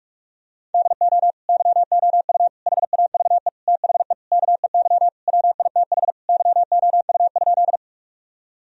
Morse code, heard as a keyed tone, about 46 wpm, 700 Hz. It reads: DO YOU HAVE THE KEY WITH YOU?
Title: Sentences from Top 1000 Words - Review Part 4 - 45wpm